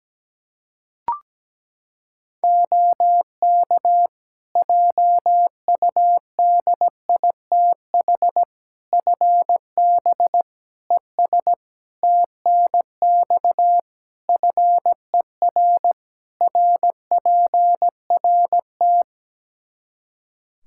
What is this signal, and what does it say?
Morse code, heard as a keyed tone, about 17 wpm, 700 Hz. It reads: OK JUDITH FB ES TNX FER RPRT
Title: Practice QSO - Exchange 2 - Info - 17wpm